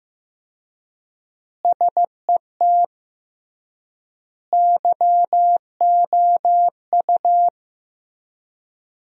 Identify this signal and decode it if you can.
Morse code, heard as a keyed tone, about 15 wpm, 700 Hz. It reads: SET YOU
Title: 3x Word-Spacing - Sentences from Top 200 Words - 15wpm